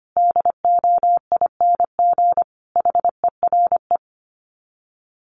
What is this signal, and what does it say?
Morse code, heard as a keyed tone, about 25 wpm, 700 Hz. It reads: DOING HERE
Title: Sets of 5 Words 25wpm